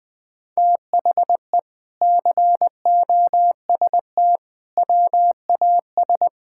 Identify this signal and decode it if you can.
Morse code, heard as a keyed tone, about 20 wpm, 700 Hz. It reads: THE COST WAS
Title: Sentences from Top 1000 Words - Review Part 6 - 20wpm